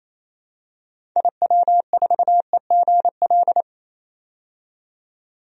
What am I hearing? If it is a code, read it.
Morse code, heard as a keyed tone, about 28 wpm, 700 Hz. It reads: IW4EGL
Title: DXCC-100 Call Signs - 6 Characters - 28wpm